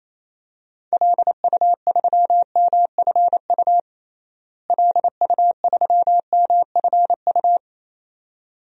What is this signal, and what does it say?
Morse code, heard as a keyed tone, about 28 wpm, 700 Hz. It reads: LU3MFU LU3MFU
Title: DXCC-100 Call Signs - 6 Characters - Repeated 2x - 28wpm